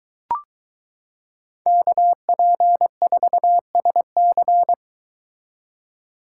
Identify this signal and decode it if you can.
Morse code, heard as a keyed tone, about 22 wpm, 700 Hz. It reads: KP4SC